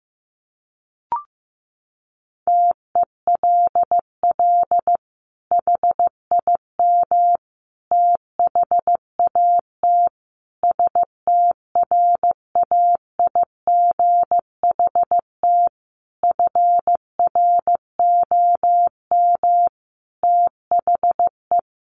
Morse code, heard as a keyed tone, about 15 wpm, 700 Hz. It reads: TELL HIM THAT STRAIGHT FROM THE
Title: Sentences from Top 1500 Words 15wpm